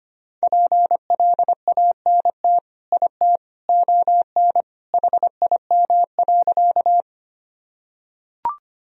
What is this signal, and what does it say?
Morse code, heard as a keyed tone, about 25 wpm, 700 Hz. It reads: PLANT IT ON HIM.
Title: Sentences from Top 1000 Words - Review Part 1 - 25wpm